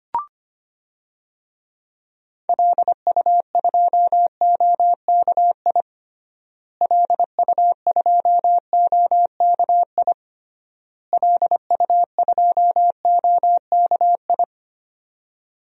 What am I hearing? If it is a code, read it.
Morse code, heard as a keyed tone, about 25 wpm, 690 Hz. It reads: LU2OKI LU2OKI LU2OKI
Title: DXCC-100 Call Signs - 6 Characters - Repeated 3x - 25wpm